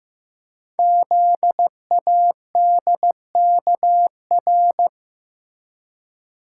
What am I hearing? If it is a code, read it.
Morse code, heard as a keyed tone, about 15 wpm, 700 Hz. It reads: ZADKR
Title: Code Groups - 5 Letters - 15wpm